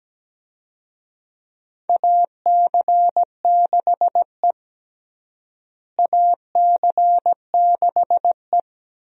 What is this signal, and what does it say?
Morse code, heard as a keyed tone, about 17 wpm, 700 Hz. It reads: AC6E AC6E
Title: USA Call Signs - 4 Characters - Repeated 2x - 17wpm